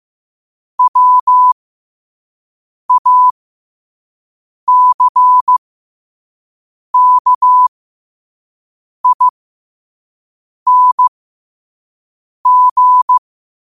Morse code, heard as a keyed tone, about 15 wpm, spaced as for 5 wpm, 1000 Hz. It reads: WACKING
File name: qotdfiles / 5wpm